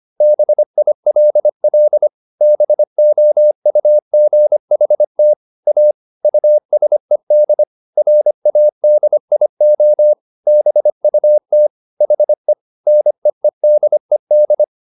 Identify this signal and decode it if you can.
Morse code, heard as a keyed tone, about 25 wpm, 600 Hz. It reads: BILL BOUGHT A USED RADIO BUT HE NEEDED